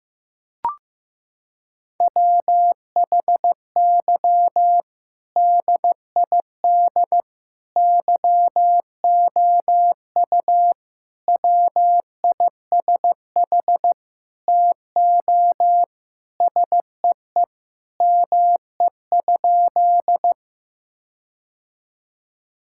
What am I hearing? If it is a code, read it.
Morse code, heard as a keyed tone, about 15 wpm, 700 Hz. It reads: WHY DID YOU WISH TO SEE ME?
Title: Sentences from Top 1600 Words 15wpm